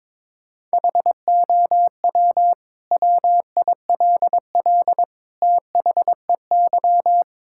Morse code, heard as a keyed tone, about 21 wpm, 700 Hz. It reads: HOW WILL THEY
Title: Sentences from Top 1000 Words - Review Part 5 - 22wpm